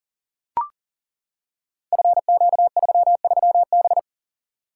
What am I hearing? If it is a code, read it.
Morse code, heard as a keyed tone, about 39 wpm, 700 Hz. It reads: FQ33B
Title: Code Groups - 5 Letter:Numbers - 40wpm